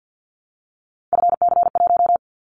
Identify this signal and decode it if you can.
Morse code, heard as a keyed tone, about 50 wpm, 700 Hz. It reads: FC1
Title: Callsign Prefixes for Top 50 DXCC - 50wpm